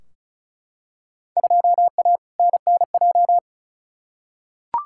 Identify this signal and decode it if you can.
Morse code, heard as a keyed tone, about 36 wpm, 700 Hz. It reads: 2A NNJ
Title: Field Day Exchanges - 35wpm